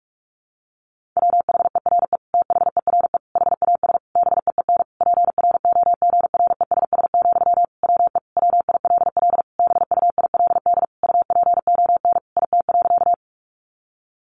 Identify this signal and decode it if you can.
Morse code, heard as a keyed tone, about 44 wpm, 700 Hz. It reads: WHERE THERE HAS BEEN PROGRESS, WE WILL BUILD UPON IT.
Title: Sentences from Top 1400 Words 45wpm